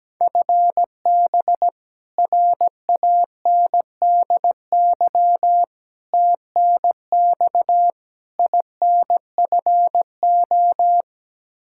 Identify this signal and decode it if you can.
Morse code, heard as a keyed tone, about 17 wpm, 700 Hz. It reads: FB RANDY TNX INFO
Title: K1USN SST QSO Practice 17wpm